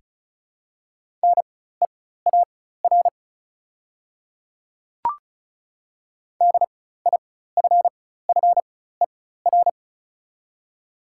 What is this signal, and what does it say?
Morse code, heard as a keyed tone, about 36 wpm, spaced as for 16 wpm, 700 Hz. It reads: NEAR DIFFER
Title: 4x Character Spacing - Top 300 Words - 35wpm